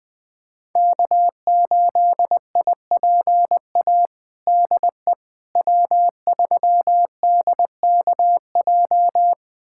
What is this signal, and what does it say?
Morse code, heard as a keyed tone, about 20 wpm, 700 Hz. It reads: K8IPA DE W3DKJ
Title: USA Callsign DE Callsign - 5 Characters - 20wpm